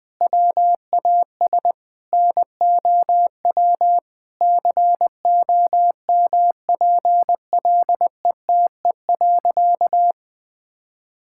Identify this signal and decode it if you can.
Morse code, heard as a keyed tone, about 20 wpm, 700 Hz. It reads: WAS NOW COMPLETE.
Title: Sentences from Top 1000 Words - Review Part 6 - 20wpm